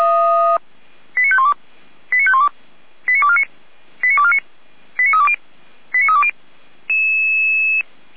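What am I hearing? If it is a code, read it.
Five-tone selective call (ZVEI), heard as quick runs of five tones, tones of about 70 ms. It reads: 89512 89512 89249 89249 89320 89320
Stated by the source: ZVEI five-tone call received by SDR, from the server of Feuerwehr Orlamünde (audio.f85360.feuerwehr-orlamuende.org).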